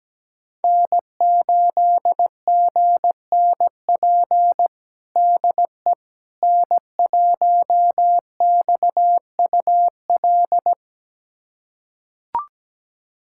Morse code, heard as a keyed tone, about 17 wpm, 700 Hz. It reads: N8GNP DE N1XUL
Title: USA Callsign DE Callsign - 5 Characters - 17wpm